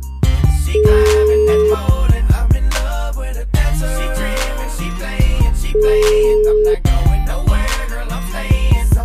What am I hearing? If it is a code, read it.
Ringback tone (Europe, 425 Hz), with noise.